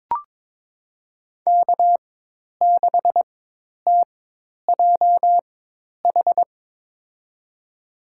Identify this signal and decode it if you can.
Morse code, heard as a keyed tone, about 22 wpm, spaced as for 10 wpm, 700 Hz. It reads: K6TJH
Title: USA Call Signs - 5 Characters - 4x Character Spacing - 22wpm